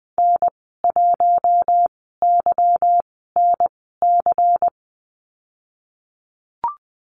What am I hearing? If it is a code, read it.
Morse code, heard as a keyed tone, about 20 wpm, spaced as for 15 wpm, 700 Hz. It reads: N1YNC
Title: 147 Lesson 24g - Letter y - Call Signs 20wpm